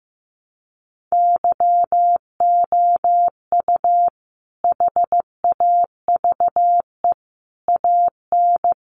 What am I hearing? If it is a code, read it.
Morse code, heard as a keyed tone, about 15 wpm, 700 Hz. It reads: YOU HAVE AN
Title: Sentences from Top 700 Words 15wpm